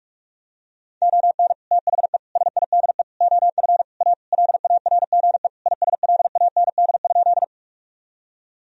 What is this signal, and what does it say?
Morse code, heard as a keyed tone, about 46 wpm, 700 Hz. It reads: ON THE SIDE OF A LARGE ISLAND?
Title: Sentences from Top 1000 Words - Review Part 3 - 45wpm